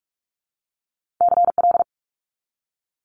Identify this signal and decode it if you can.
Morse code, heard as a keyed tone, about 46 wpm, 690 Hz. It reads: CL
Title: Multi-letter Phonemes 45wpm